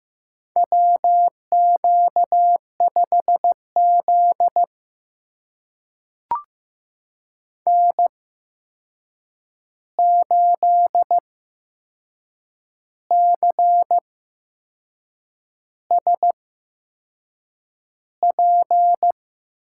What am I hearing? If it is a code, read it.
Morse code, heard as a keyed tone, about 15 wpm, spaced as for 6 wpm, 700 Hz. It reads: WQ5Z N8CSP
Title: USA Call Signs - 8x Character Spacing - 15wpm